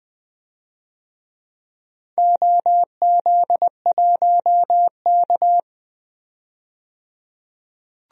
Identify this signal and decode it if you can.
Morse code, heard as a keyed tone, about 20 wpm, 700 Hz. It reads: OZ1K
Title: DXCC-100 Call Signs - 4 Characters - 20wpm